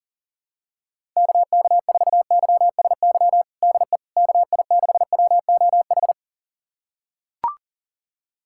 Morse code, heard as a keyed tone, about 40 wpm, 700 Hz. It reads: KK4YSY DE KI6WOH
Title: USA Callsign DE Callsign 40wpm